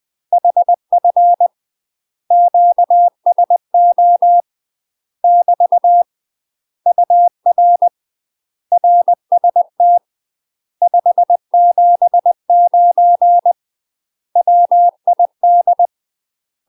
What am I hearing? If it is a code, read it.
Morse code, heard as a keyed tone, about 20 wpm, 700 Hz. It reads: HF QSO <BT> UR RST 579 WID
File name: UK MASTER QSOs TRAINING / 20WPM / UK MASTER QSOs 0005